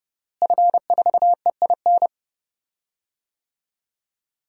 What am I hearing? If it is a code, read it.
Morse code, heard as a keyed tone, about 30 wpm, 700 Hz. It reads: F4EIN
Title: DXCC-100 Call Signs - 5 Characters - Repeated 2x - 30wpm